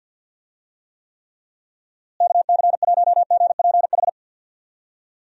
Morse code, heard as a keyed tone, about 51 wpm, 700 Hz. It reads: KC1GPH
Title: USA Call Signs - 6 Characters - Repeated 3x - 50wpm